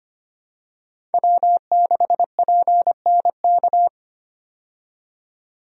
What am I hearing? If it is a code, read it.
Morse code, heard as a keyed tone, about 25 wpm, 690 Hz. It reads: W6PNK
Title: USA Call Signs - 5 Characters - 25wpm